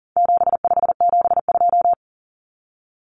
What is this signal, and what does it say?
Morse code, heard as a keyed tone, about 40 wpm, 700 Hz. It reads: 7572
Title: Numbers - 4-Digit 40wpm